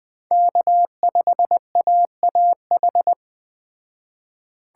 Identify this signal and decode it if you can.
Morse code, heard as a keyed tone, about 20 wpm, 700 Hz. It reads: K5AAH
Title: Speed Racing - USA 1x3 Call Signs 17wpm